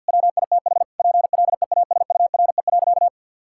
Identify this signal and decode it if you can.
Morse code, heard as a keyed tone, about 51 wpm, 700 Hz. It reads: WITH PLEASURE.
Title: Sentences from Top 1600 Words 50wpm